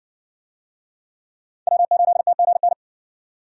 Morse code, heard as a keyed tone, about 60 wpm, 700 Hz. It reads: W8TGN